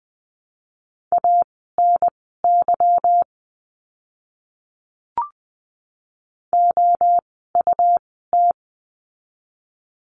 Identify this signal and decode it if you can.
Morse code, heard as a keyed tone, about 20 wpm, spaced as for 14 wpm, 700 Hz. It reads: ANY OUT
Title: 244 Lesson 39f - Letter z - Words 20wpm